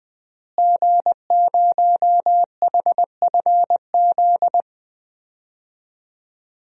Morse code, heard as a keyed tone, about 20 wpm, 700 Hz. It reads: G0HFZ
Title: DXCC-100 Call Signs - 5 Characters - Repeated 3x - 20wpm